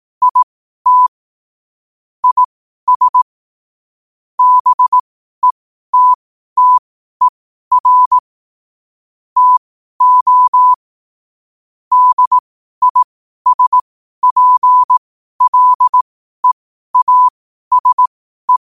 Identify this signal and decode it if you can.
Morse code, heard as a keyed tone, about 18 wpm, spaced as for 12 wpm, 1000 Hz. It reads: IT IS BETTER TO DISPLEASE
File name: qotdfiles / 12wpm